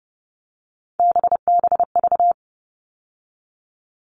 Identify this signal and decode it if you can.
Morse code, heard as a keyed tone, about 30 wpm, 700 Hz. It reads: BBV
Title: All Cut Numbers - 3-Digit 30wpm